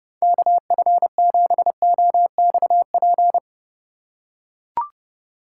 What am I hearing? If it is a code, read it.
Morse code, heard as a keyed tone, about 30 wpm, 700 Hz. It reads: KF7OXP
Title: USA Call Signs - 3x Character Spacing - 30wpm